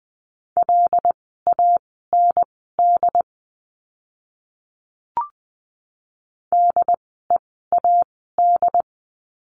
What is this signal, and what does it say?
Morse code, heard as a keyed tone, about 20 wpm, spaced as for 14 wpm, 700 Hz. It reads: LAND DEAD
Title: 075 Lesson 13f - Letter L - Words 20wpm